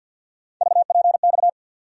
Morse code, heard as a keyed tone, about 51 wpm, 700 Hz. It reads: VPX
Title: USA Call Sign Postfixes - 3 Letters 50wpm